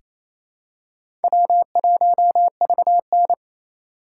Morse code, heard as a keyed tone, about 28 wpm, 700 Hz. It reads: W1VN
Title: USA Call Signs - 4 Characters - 2x Character Spacing - 28wpm